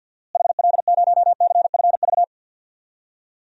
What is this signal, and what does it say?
Morse code, heard as a keyed tone, about 53 wpm, 700 Hz. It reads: HL0CFV
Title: DXCC-100 Call Signs - 6 Characters - Repeated 2x - 50wpm